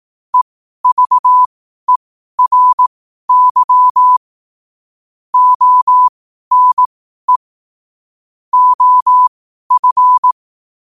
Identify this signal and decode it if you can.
Morse code, heard as a keyed tone, about 18 wpm, spaced as for 12 wpm, 1000 Hz. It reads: EVERY ONE OF
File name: qotdfiles / 12wpm